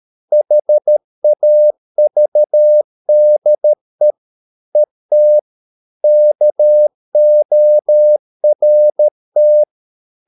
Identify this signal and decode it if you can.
Morse code, heard as a keyed tone, about 13 wpm, 600 Hz. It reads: HAVDE ET KORT
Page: Ferie 13 wpm